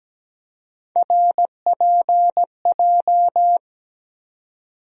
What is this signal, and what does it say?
Morse code, heard as a keyed tone, about 17 wpm, 700 Hz. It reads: RPJ
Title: USA Call Sign Postfixes - 3 Letters 17wpm